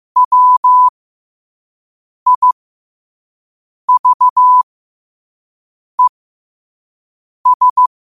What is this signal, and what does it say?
Morse code, heard as a keyed tone, about 15 wpm, spaced as for 5 wpm, 1000 Hz. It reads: WIVES